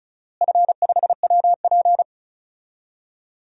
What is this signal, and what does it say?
Morse code, heard as a keyed tone, about 36 wpm, 700 Hz. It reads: F5WP
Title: DXCC-100 Call Signs - 4 Characters - Repeated 3x - 35wpm